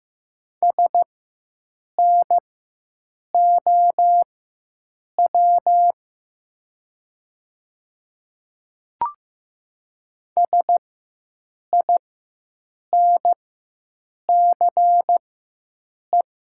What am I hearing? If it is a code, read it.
Morse code, heard as a keyed tone, about 15 wpm, spaced as for 7 wpm, 700 Hz. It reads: SNOW SINCE
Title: 4x Character Spacing - Top 500 Words - 15wpm